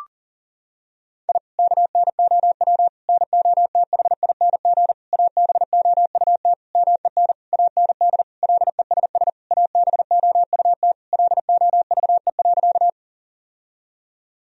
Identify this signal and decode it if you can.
Morse code, heard as a keyed tone, about 40 wpm, 700 Hz. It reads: I KNOW NOTHING ABOUT MEN AND LESS ABOUT LOVE.
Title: Sentences from Top 1000 Words - Review Part 2 - 40wpm